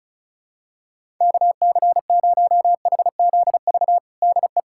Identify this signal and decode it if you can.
Morse code, heard as a keyed tone, about 36 wpm, 700 Hz. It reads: KC0HZV DE
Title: USA Callsign DE Callsign - 6 Characters - 35wpm